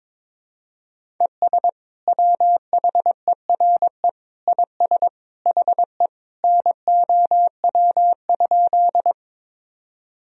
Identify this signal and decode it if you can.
Morse code, heard as a keyed tone, about 22 wpm, 700 Hz. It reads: ES WHERE IS HE NOW?
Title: Sentences from Top 200 Words and Abbreviations 22wpm